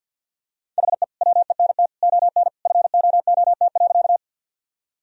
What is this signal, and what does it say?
Morse code, heard as a keyed tone, about 50 wpm, 700 Hz. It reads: HE WENT ON FOOT.
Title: Sentences from Top 1000 Words - Review Part 3 - 50wpm